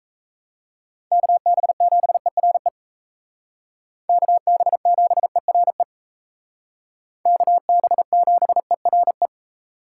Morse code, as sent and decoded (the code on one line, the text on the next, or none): -.- -... --... .-. / -.- -... --... .-. / -.- -... --... .-. .
KB7ERE KB7ERE KB7ERE